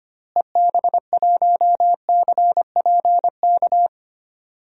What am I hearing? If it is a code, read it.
Morse code, heard as a keyed tone, about 25 wpm, 700 Hz. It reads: EB1CPK
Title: DXCC-100 Call Signs - 6 Characters - Repeated 3x - 25wpm